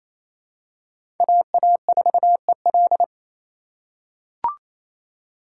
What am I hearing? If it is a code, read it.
Morse code, heard as a keyed tone, about 28 wpm, 700 Hz. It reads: AA4EL
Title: USA Call Signs - 5 Characters - 2x Character Spacing - 28wpm